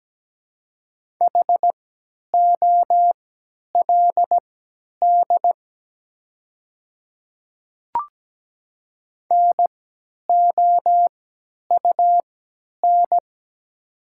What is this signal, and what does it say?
Morse code, heard as a keyed tone, about 17 wpm, spaced as for 10 wpm, 700 Hz. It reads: HOLD NOUN
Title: 3x Character Spacing - Top 500 Words - 17wpm